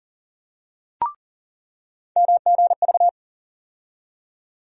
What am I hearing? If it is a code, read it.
Morse code, heard as a keyed tone, about 40 wpm, 700 Hz. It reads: MGV
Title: USA Call Sign Postfixes - 3 Letters 40wpm